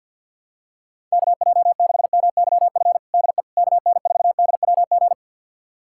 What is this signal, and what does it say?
Morse code, heard as a keyed tone, about 47 wpm, 700 Hz. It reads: KJ6MYF DE KN4DWG
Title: USA Callsign DE Callsign 50wpm